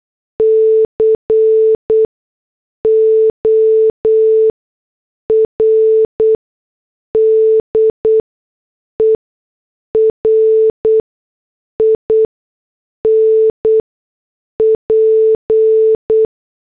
Morse code, heard as a keyed tone, about 8 wpm, spaced as for 6 wpm, 440 Hz. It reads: CORDER IN P
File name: morse / headlines